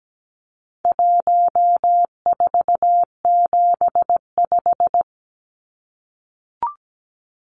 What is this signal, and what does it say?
Morse code, heard as a keyed tone, about 17 wpm, 700 Hz. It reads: 1475
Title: Numbers - 4-Digit 17wpm